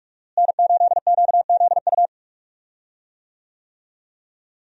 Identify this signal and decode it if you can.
Morse code, heard as a keyed tone, about 41 wpm, 700 Hz. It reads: N8QZU
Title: USA Call Signs - 5 Characters - Repeated 3x - 45wpm